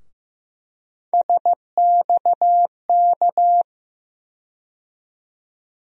Morse code, heard as a keyed tone, about 15 wpm, 700 Hz. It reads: SXK
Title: USA Call Sign Postfixes - 3 Letters 15wpm